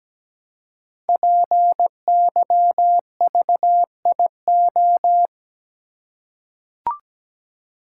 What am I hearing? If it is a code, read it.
Morse code, heard as a keyed tone, about 17 wpm, 700 Hz. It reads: PYVIO
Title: Code Groups - 5 Letter:Numbers - 17wpm